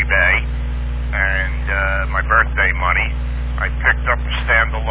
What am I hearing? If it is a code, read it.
Emergency Alert System (SAME) tones, heard as harsh buzzing bursts, with speech. It is heard from 1.0 to 4.0 s.